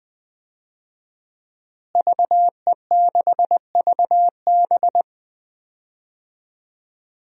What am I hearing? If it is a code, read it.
Morse code, heard as a keyed tone, about 20 wpm, 700 Hz. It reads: VE6VB
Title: DXCC-100 Call Signs - 5 Characters - Repeated 3x - 20wpm